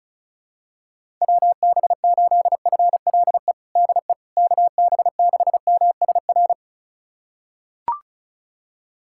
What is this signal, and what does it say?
Morse code, heard as a keyed tone, about 36 wpm, 700 Hz. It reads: WB8FLE DE KB6MSR
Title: USA Callsign DE Callsign - 6 Characters - 35wpm